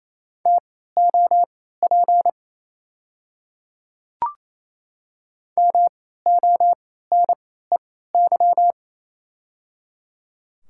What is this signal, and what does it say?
Morse code, heard as a keyed tone, about 28 wpm, spaced as for 16 wpm, 690 Hz. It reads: TOP MONEY
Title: 3x Character Spacing - Top 500 Words - 28wpm